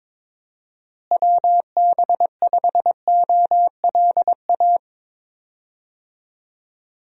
Morse code, heard as a keyed tone, about 22 wpm, 700 Hz. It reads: WB5OLA
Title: USA Call Signs - 6 Characters - Repeated 2x - 22wpm